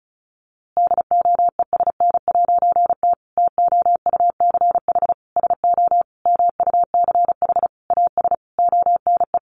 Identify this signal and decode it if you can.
Morse code, heard as a keyed tone, about 34 wpm, 700 Hz. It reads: DOESN'T TOUCH SO MUCH AS ONE
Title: A Sound Of Thunder 35wpm